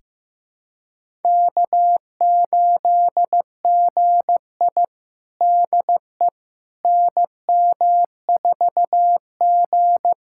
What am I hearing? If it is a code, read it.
Morse code, heard as a keyed tone, about 15 wpm, 700 Hz. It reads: K8GI DE NM4G
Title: USA Callsign DE Callsign - 4 Characters - 15wpm